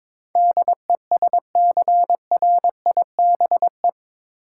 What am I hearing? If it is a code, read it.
Morse code, heard as a keyed tone, about 22 wpm, 700 Hz. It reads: DESCRIBE